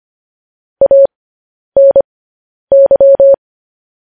Morse code, heard as a keyed tone, about 25 wpm, spaced as for 10 wpm, 550 Hz. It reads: ANY
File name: podcast / News headlines-1x-25x10